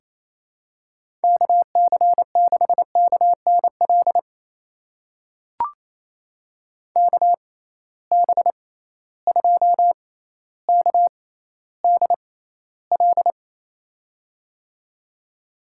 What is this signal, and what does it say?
Morse code, heard as a keyed tone, about 28 wpm, spaced as for 14 wpm, 700 Hz. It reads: KC6KNL KB2KDL
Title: USA Call Signs - 6 Characters - 6x Character Spacing - 28wpm